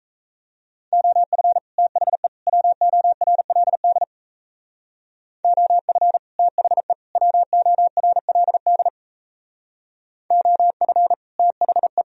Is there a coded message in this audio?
Yes: Morse code from 0.9 to 12.0 s, heard as a keyed tone, about 38 wpm, 700 Hz. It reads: OF THE WORLD OF THE WORLD OF THE